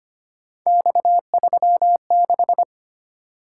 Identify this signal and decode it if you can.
Morse code, heard as a keyed tone, about 25 wpm, 700 Hz. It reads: X36
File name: Three Letter-Number Combinations - 25wpm